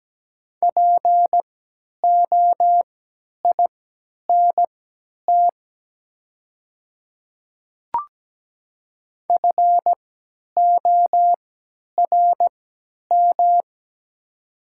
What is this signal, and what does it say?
Morse code, heard as a keyed tone, about 17 wpm, spaced as for 10 wpm, 700 Hz. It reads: POINT FORM
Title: 3x Character Spacing - Top 200 Words - 17wpm